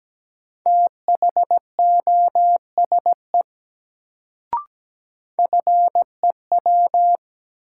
Morse code, heard as a keyed tone, about 17 wpm, 700 Hz. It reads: THOSE FEW
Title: Sentences from Top 300 Words - New Word Drill - 17wpm